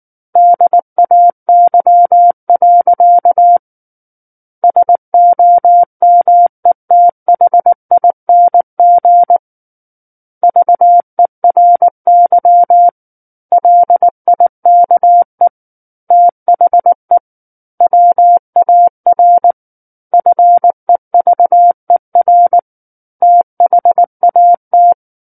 Morse code, heard as a keyed tone, about 19 wpm, 700 Hz. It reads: DAY. SOMETHING VERY LIKE THE WAR FEVER THAT
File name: Book One / 10-Chapter 10 - 19 WPM